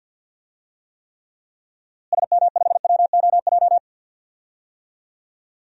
Morse code, heard as a keyed tone, about 48 wpm, 700 Hz. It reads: SM5WOJ
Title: DXCC-100 Call Signs - 6 Characters - Repeated 3x - 50wpm